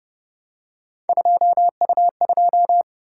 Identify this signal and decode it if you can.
Morse code, heard as a keyed tone, about 28 wpm, 700 Hz. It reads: 2U2